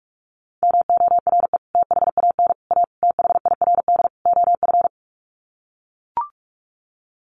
Morse code, heard as a keyed tone, about 44 wpm, 700 Hz. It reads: MORE THAN A THIRD OF
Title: Sets of 5 Words Encore2 45wpm